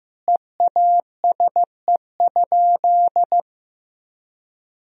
Morse code, heard as a keyed tone, about 15 wpm, 700 Hz. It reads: EASE?
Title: Sentences from Top 1000 Words - Review Part 1 - 15wpm